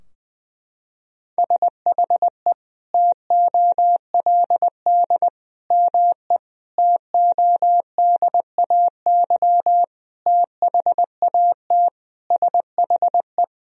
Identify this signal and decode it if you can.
Morse code, heard as a keyed tone, about 20 wpm, 700 Hz. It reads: SHE TOLD ME TODAY THAT SHE
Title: Sentences from Top 1000 Words - Review Part 4 - 20wpm